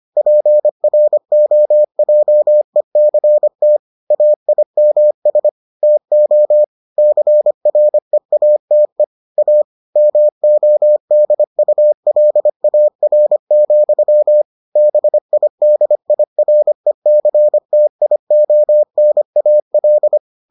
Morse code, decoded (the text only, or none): PROJECT AIMS TO CREATE A MODULAR, BIDIRECTIONAL